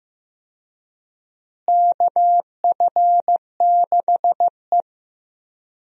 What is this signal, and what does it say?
Morse code, heard as a keyed tone, about 15 wpm, 700 Hz. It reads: KF6E
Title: USA Call Signs - 4 Characters - Repeated 2x - 15wpm